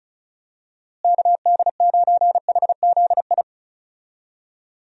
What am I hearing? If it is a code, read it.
Morse code, heard as a keyed tone, about 34 wpm, 700 Hz. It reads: KD9HZI
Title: USA Call Signs - 6 Characters - Repeated 2x - 35wpm